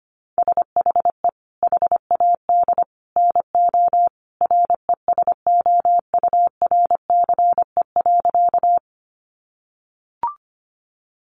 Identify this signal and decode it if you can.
Morse code, heard as a keyed tone, about 25 wpm, 700 Hz. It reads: SHE HAD NO RESOURCE.
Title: Sentences from Top 1300 Words 25wpm